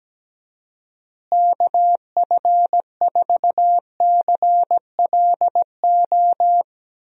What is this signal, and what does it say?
Morse code, heard as a keyed tone, about 17 wpm, 700 Hz. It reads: KF4CLO
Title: USA Call Signs - 6 Characters - Repeated 3x - 17wpm